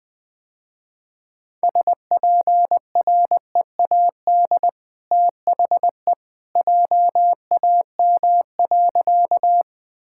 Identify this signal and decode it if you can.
Morse code, heard as a keyed tone, about 20 wpm, 700 Hz. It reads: SPREAD THE JAM.
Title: Easy Sentences with Hints - 20wpm